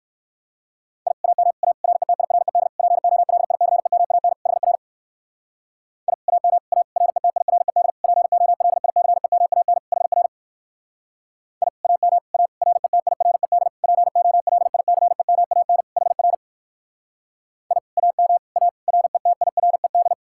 Morse code, heard as a keyed tone, about 65 wpm, 700 Hz. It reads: I AM A RETIRED POLICEMAN HR I AM A RETIRED POLICEMAN HR I AM A RETIRED POLICEMAN HR I AM A RETIRED